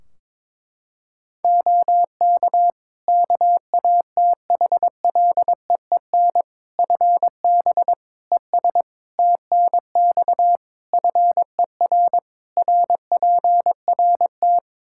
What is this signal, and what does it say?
Morse code, heard as a keyed tone, about 21 wpm, 700 Hz. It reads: OK KATHLEEN FB ES TNX FER RPRT
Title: Practice QSO - Exchange 2 - Info - 22wpm